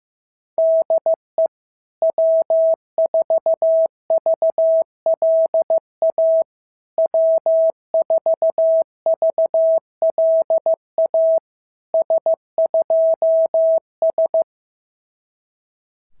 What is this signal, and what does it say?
Morse code, heard as a keyed tone, about 14 wpm, 650 Hz. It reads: DE W4VLA W4VLA S2S
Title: SOTA QSO Practice 15wpm